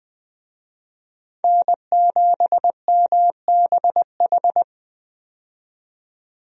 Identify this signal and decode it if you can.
Morse code, heard as a keyed tone, about 20 wpm, 700 Hz. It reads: N7MBH
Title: USA Call Signs - 5 Characters - 20wpm